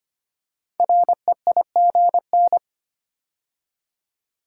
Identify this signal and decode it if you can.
Morse code, heard as a keyed tone, about 25 wpm, 700 Hz. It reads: REIGN
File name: 5 Letter Words - First 3 Letters Given - 25wpm